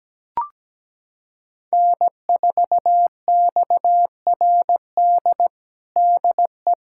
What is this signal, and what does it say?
Morse code, heard as a keyed tone, about 17 wpm, 700 Hz. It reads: N4XRD DE
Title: USA Callsign DE Callsign 17wpm